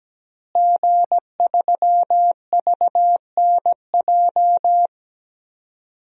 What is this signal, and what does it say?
Morse code, heard as a keyed tone, about 17 wpm, 700 Hz. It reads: G3VNJ
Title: DXCC-100 Call Signs - 5 Characters - Repeated 3x - 17wpm